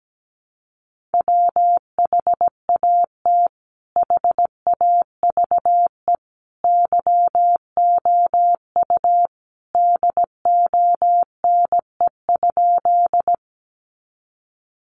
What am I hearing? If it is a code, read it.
Morse code, heard as a keyed tone, about 17 wpm, 700 Hz. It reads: WHAT HAVE YOU DONE?
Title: Sentences from Top 500 Words 17wpm